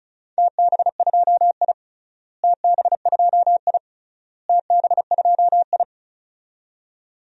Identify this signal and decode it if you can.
Morse code, heard as a keyed tone, about 34 wpm, 700 Hz. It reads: TB2I TB2I TB2I
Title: DXCC-100 Call Signs - 4 Characters - Repeated 3x - 35wpm